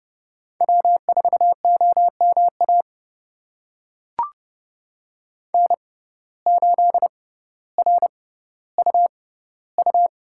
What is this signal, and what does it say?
Morse code, heard as a keyed tone, about 31 wpm, spaced as for 15 wpm, 700 Hz. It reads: W4OMA N8RUU
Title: USA Call Signs - 5 Characters - 6x Character Spacing - 30wpm